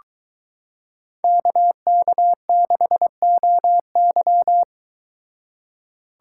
Morse code, heard as a keyed tone, about 22 wpm, 700 Hz. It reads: KK6OY